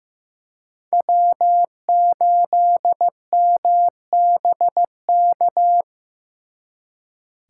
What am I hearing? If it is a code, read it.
Morse code, heard as a keyed tone, about 18 wpm, 700 Hz. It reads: W8MBK
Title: Speed Racing - USA 1x3 Call Signs 15wpm